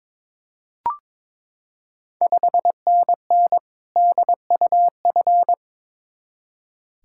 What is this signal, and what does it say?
Morse code, heard as a keyed tone, about 21 wpm, 690 Hz. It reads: 5NN DUF
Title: Ontario QSO Party Exchange 22wpm